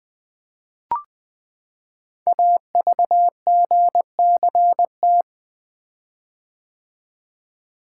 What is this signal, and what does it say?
Morse code, heard as a keyed tone, about 20 wpm, 700 Hz. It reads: AVGCT
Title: Code Groups - 5 Letter:Numbers - 20wpm